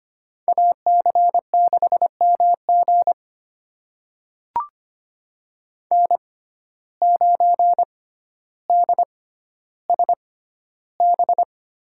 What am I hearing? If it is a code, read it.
Morse code, heard as a keyed tone, about 25 wpm, spaced as for 13 wpm, 700 Hz. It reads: AC6MG N9DSB
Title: USA Call Signs - 5 Characters - 6x Character Spacing - 25wpm